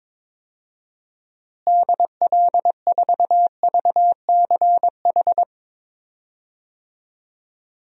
Morse code, heard as a keyed tone, about 22 wpm, 700 Hz. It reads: DL4VCH
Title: DXCC-100 Call Signs - 6 Characters - Repeated 2x - 22wpm